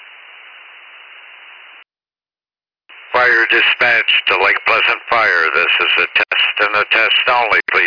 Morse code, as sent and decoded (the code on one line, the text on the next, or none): none